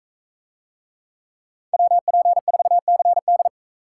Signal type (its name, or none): Morse code